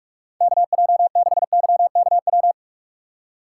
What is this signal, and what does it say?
Morse code, heard as a keyed tone, about 46 wpm, 690 Hz. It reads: KJ6YKW
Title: USA Call Signs - 6 Characters - 2x Character Spacing - 45wpm